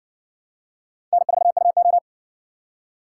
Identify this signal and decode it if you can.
Morse code, heard as a keyed tone, about 56 wpm, 700 Hz. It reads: N4UO